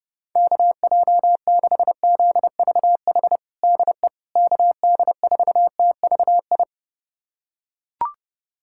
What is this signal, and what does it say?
Morse code, heard as a keyed tone, about 30 wpm, 700 Hz. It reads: KJ6ZVH DE KD4TVI
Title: USA Callsign DE Callsign 30wpm